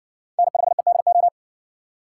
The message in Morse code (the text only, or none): N5EDO